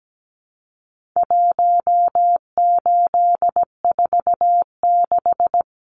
Morse code, heard as a keyed tone, about 17 wpm, 700 Hz. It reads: 1846
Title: Numbers - Contest Style - 4-Digit 17wpm